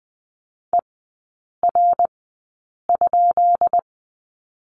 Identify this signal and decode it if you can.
Morse code, heard as a keyed tone, about 19 wpm, spaced as for 8 wpm, 690 Hz. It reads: ER?
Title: 166 Lesson 27e - Number 7 - Three Characters 20wpm